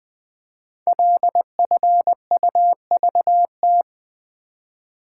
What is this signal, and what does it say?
Morse code, heard as a keyed tone, about 20 wpm, 700 Hz. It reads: LFUVT
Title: Code Groups - 5 Letters - 20wpm